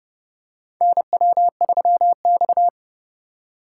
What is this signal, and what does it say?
Morse code, heard as a keyed tone, about 31 wpm, 700 Hz. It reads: NW3X
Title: USA Call Signs - 4 Characters - 6x Character Spacing - 30wpm